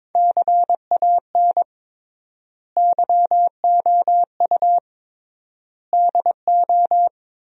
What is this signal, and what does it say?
Morse code, heard as a keyed tone, about 21 wpm, 700 Hz. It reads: CAN YOU DO
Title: 3x Word-Spacing - Sentences from Top 100 Words - 22wpm